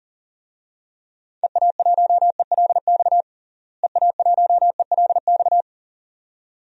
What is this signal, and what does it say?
Morse code, heard as a keyed tone, about 40 wpm, 700 Hz. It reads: EA1ELX EA1ELX
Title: DXCC-100 Call Signs - 6 Characters - Repeated 2x - 40wpm